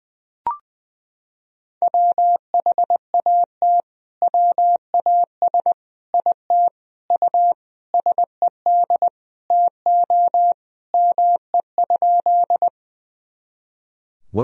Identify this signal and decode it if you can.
Morse code, heard as a keyed tone, about 21 wpm, 700 Hz. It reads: WHAT WAS IT U SED TO ME?
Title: Sentences from Top 200 Words and Abbreviations 20wpm